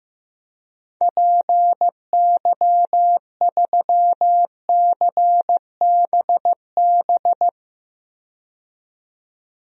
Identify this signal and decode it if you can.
Morse code, heard as a keyed tone, about 15 wpm, 700 Hz. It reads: PY3CBB
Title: DXCC-100 Call Signs - 6 Characters - 15wpm